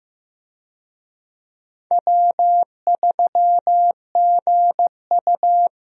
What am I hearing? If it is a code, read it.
Morse code, heard as a keyed tone, about 15 wpm, 700 Hz. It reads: W3GU
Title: USA Call Signs - 4 Characters - 15wpm